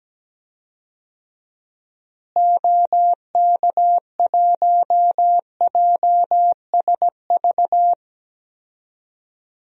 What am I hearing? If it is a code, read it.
Morse code, heard as a keyed tone, about 17 wpm, 700 Hz. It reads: OK1JSV
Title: DXCC-100 Call Signs - 6 Characters - Repeated 2x - 17wpm